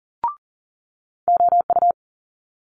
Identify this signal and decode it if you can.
Morse code, heard as a keyed tone, about 40 wpm, 700 Hz. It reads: OU
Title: Multi-letter Phonemes Quick Fire 40wpm